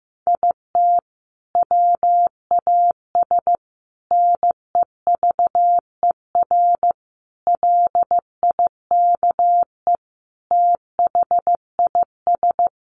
Morse code, heard as a keyed tone, about 15 wpm, 700 Hz. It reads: IT WAS NEVER LIKE THIS
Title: A Sound Of Thunder 15wpm